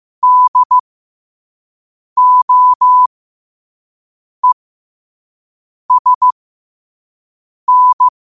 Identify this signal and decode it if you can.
Morse code, heard as a keyed tone, about 15 wpm, spaced as for 5 wpm, 1000 Hz. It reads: DOESN
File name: qotdfiles / 5wpm